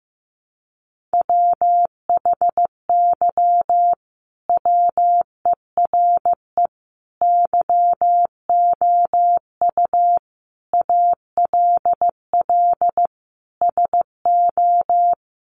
Morse code, heard as a keyed tone, about 14 wpm, 700 Hz. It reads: WHY WERE YOU ALL SO
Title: Sentences from Top 1400 Words 15wpm